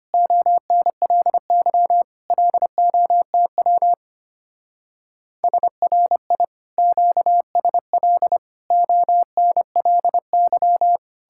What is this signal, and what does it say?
Morse code, heard as a keyed tone, about 33 wpm, 700 Hz. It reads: ONLY LOTW SRI QSL ONLY